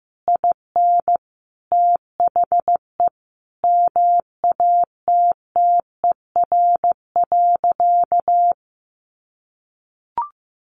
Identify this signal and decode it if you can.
Morse code, heard as a keyed tone, about 15 wpm, 700 Hz. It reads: IN THE MATTER.
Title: Sentences from Top 1200 Words 15wpm